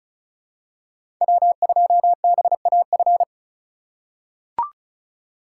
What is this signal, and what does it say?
Morse code, heard as a keyed tone, about 34 wpm, 700 Hz. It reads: W2BAF
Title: USA Call Signs - 5 Characters - 3x Character Spacing - 35wpm